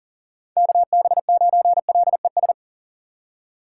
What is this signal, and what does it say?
Morse code, heard as a keyed tone, about 40 wpm, 690 Hz. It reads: KB9LES
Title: USA Call Signs - 3x Character Spacing - 40wpm